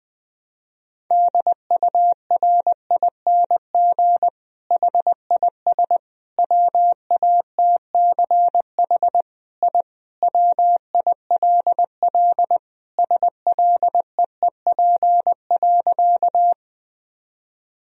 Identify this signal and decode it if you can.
Morse code, heard as a keyed tone, about 20 wpm, 700 Hz. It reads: DURING HIS WATCH I WILL SLEEP.
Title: Sentences from Top 1000 Words - Review Part 2 - 20wpm